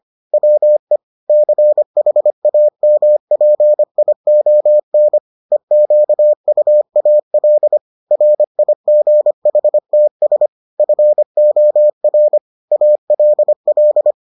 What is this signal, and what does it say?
Morse code, heard as a keyed tone, about 25 wpm, 600 Hz. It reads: WE CHAMPION EQUAL RIGHTS FOR ALL